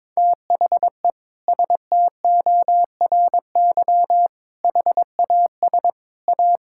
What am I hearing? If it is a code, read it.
Morse code, heard as a keyed tone, about 22 wpm, 700 Hz. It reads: THE STORY HAS A
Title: Sentences from Top 1000 Words - Review Part 6 - 22wpm